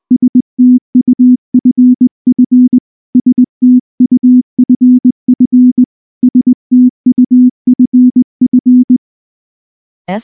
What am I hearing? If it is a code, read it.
Morse code, heard as a keyed tone, about 19 wpm, 250 Hz. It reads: STUFF STUFF STUFF
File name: morse / morse-training-20WPM-2